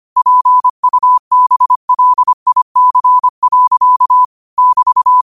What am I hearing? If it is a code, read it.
Morse code, heard as a keyed tone, about 25 wpm, 1000 Hz. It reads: PUBLIC. <BT>
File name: qotdfiles / 25wpm